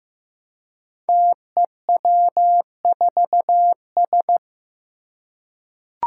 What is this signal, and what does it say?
Morse code, heard as a keyed tone, about 15 wpm, 700 Hz. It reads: TEW4S
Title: Five Challenging Letter-Number Combinations 15wpm